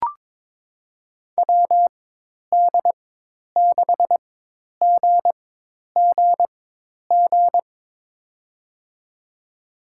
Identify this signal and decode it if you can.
Morse code, heard as a keyed tone, about 22 wpm, spaced as for 10 wpm, 700 Hz. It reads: WD6GGG
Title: USA Call Signs - 4x Character Spacing - 22wpm